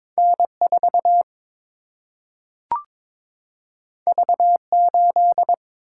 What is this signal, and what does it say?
Morse code, heard as a keyed tone, about 22 wpm, 700 Hz. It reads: N 4 V 8
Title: Single-Letter Single-Number 22wpm